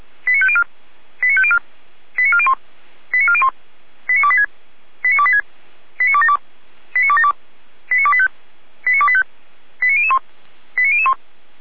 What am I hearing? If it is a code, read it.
Five-tone selective call (ZVEI), heard as quick runs of five tones, tones of about 70 ms. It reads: 89504 89504 89401 89401 89286 89286 89272 89272 89285 89285 89001 89001
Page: /455/SDR/ZVEI/89340/2021.10.09/